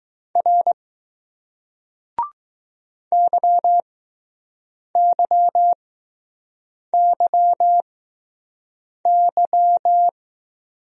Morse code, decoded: R YYYY

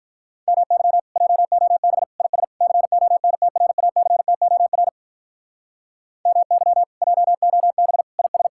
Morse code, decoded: MY JOB IS CONTRACTOR MY JOB IS

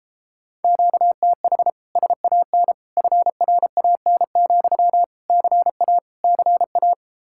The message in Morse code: QTH SAN FRAN, CA CA

